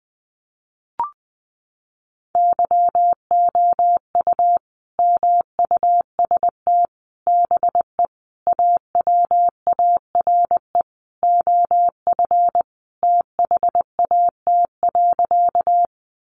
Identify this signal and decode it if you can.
Morse code, heard as a keyed tone, about 19 wpm, 700 Hz. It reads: YOU MUST BE AWARE OF THAT.
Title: Sentences from Top 1200 Words 20wpm